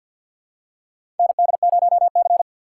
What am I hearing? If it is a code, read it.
Morse code, heard as a keyed tone, about 53 wpm, 700 Hz. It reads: ND0C
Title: USA Call Signs - 4 Characters - 6x Character Spacing - 50wpm